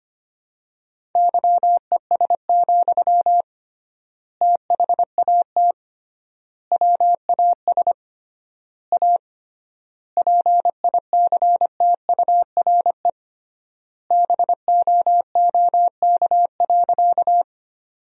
Morse code, heard as a keyed tone, about 25 wpm, 700 Hz. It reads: YES, THAT WAS A PICTURE BOOK.
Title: 3x Word-Spacing - Sentences from Top 500 Words - 25wpm